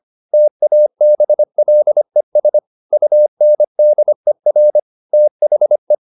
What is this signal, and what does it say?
Morse code, heard as a keyed tone, about 25 wpm, 600 Hz. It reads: TABLES UNDER THE